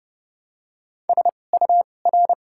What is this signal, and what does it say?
Morse code, heard as a keyed tone, about 29 wpm, spaced as for 21 wpm, 700 Hz. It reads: SUR
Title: 2x Character Spacing - Top 500 Words - 30wpm